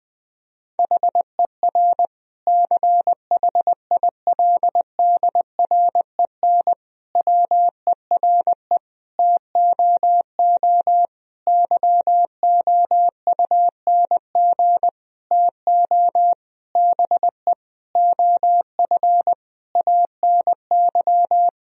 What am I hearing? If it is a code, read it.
Morse code, heard as a keyed tone, about 20 wpm, 700 Hz. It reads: HER CHILDREN WERE TOO YOUNG TO BE OF ANY
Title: Sentences from Top 1000 Words - Review Part 4 - 20wpm